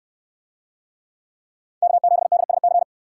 Morse code, heard as a keyed tone, about 69 wpm, 700 Hz. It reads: K7DSC